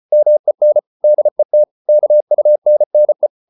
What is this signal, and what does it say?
Morse code, heard as a keyed tone, about 34 wpm, 600 Hz. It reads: MEN DET KUNNE
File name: Ferie QRQ 34wpm